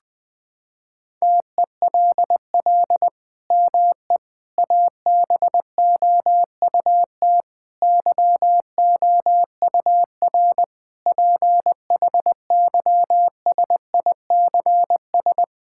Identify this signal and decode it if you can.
Morse code, heard as a keyed tone, about 20 wpm, 700 Hz. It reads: TELL ME ABOUT YOUR PHYSICS
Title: Sentences from Top 1000 Words - Review Part 5 - 20wpm